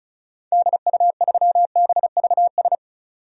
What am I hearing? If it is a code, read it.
Morse code, heard as a keyed tone, about 36 wpm, 700 Hz. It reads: DU3BVS
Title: DXCC-100 Call Signs - 6 Characters - 35wpm